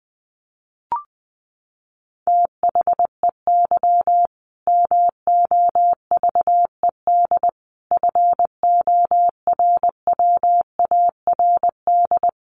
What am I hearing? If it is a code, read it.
Morse code, heard as a keyed tone, about 20 wpm, 700 Hz. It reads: THEY MOVED FORWARD
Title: A Sound Of Thunder 20wpm